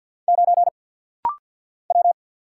Morse code, heard as a keyed tone, about 50 wpm, 690 Hz. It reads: 9 W
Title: Single Letter-Number - Rapid-Fire 50wpm